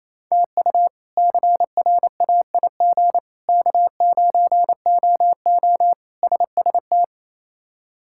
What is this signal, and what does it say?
Morse code, heard as a keyed tone, about 28 wpm, 700 Hz. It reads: TU CRAIG K9OO SST
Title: K1USN SST QSO Practice - 28wpm